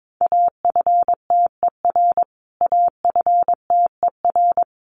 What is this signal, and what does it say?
Morse code, heard as a keyed tone, about 22 wpm, 700 Hz. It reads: AFTER AFTER
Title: Top 500 Words Repeated 22wpm